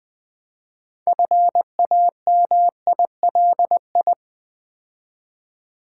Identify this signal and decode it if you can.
Morse code, heard as a keyed tone, about 20 wpm, 700 Hz. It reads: FAMILI
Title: Build A Word - 2 Letters At A Time - 8-Letter Words - 20wpm